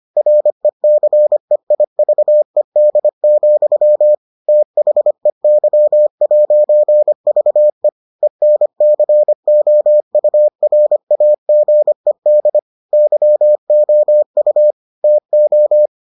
Morse code, heard as a keyed tone, about 25 wpm, 600 Hz. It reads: RECEIVED, THEY'VE ENCOURAGED YOU TO